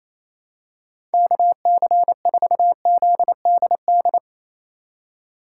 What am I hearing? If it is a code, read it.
Morse code, heard as a keyed tone, about 28 wpm, 700 Hz. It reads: KC4ZDD
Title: USA Call Signs - 6 Characters - 4x Character Spacing - 28wpm